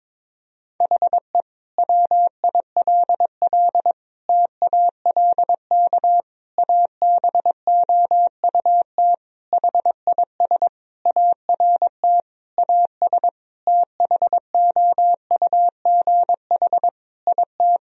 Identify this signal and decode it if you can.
Morse code, heard as a keyed tone, about 21 wpm, 700 Hz. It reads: HE WILL TALK ABOUT HIS ART AS THOUGH IT
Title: Sentences from Top 1000 Words - Review Part 3 - 22wpm